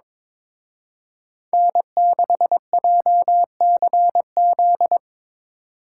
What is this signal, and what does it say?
Morse code, heard as a keyed tone, about 26 wpm, 700 Hz. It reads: N6JCZ
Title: Speed Racing - USA 1x3 Call Signs 22wpm